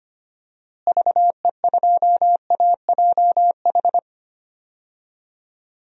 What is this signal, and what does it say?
Morse code, heard as a keyed tone, about 25 wpm, 700 Hz. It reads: VE2AJH
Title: DXCC-100 Call Signs - 6 Characters - Repeated 2x - 25wpm